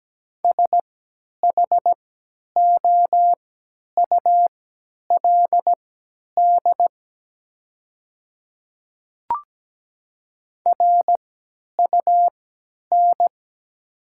Morse code, heard as a keyed tone, about 17 wpm, spaced as for 10 wpm, 700 Hz. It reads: SHOULD RUN
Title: 3x Character Spacing - Top 300 Words - 17wpm